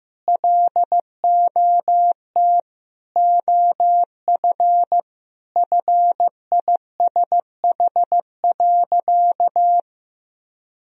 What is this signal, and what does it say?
Morse code, heard as a keyed tone, about 16 wpm, 700 Hz. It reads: LOT OF FISH.
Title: Sentences from Top 1000 Words - Review Part 3 - 15wpm